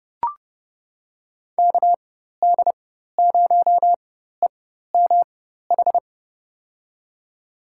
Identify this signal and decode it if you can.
Morse code, heard as a keyed tone, about 30 wpm, spaced as for 14 wpm, 700 Hz. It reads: KD0EMH
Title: USA Call Signs - 6 Characters - 4x Character Spacing - 30wpm